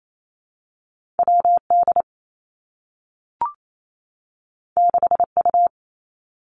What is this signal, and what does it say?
Morse code, heard as a keyed tone, about 28 wpm, 700 Hz. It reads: W D 6 U
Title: Two Challenging Letter-Number Combinations 28wpm